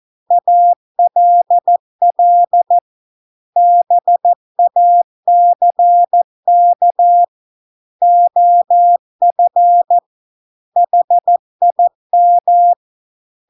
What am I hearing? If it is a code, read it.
Morse code, heard as a keyed tone, about 14 wpm, 700 Hz. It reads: ALL BACK OF HIM